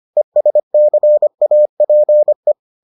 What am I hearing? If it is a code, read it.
Morse code, heard as a keyed tone, about 25 wpm, 600 Hz. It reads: ESCAPE